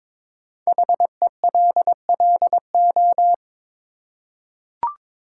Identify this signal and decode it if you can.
Morse code, heard as a keyed tone, about 22 wpm, 700 Hz. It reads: HELLO
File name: 5 Letter Words - First 3 Letters Given - 22wpm